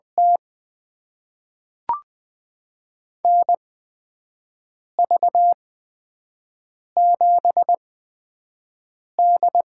Morse code, heard as a keyed tone, about 20 wpm, spaced as for 5 wpm, 700 Hz. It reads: T NV7D
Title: USA Call Signs - 4 Characters - 8x Character Spacing - 20wpm